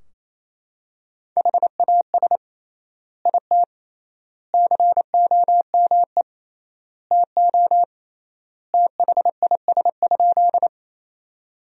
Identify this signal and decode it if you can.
Morse code, heard as a keyed tone, about 28 wpm, 700 Hz. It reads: HAS IT COME TO THIS?
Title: 3x Word-Spacing - Sentences from Top 100 Words - 28wpm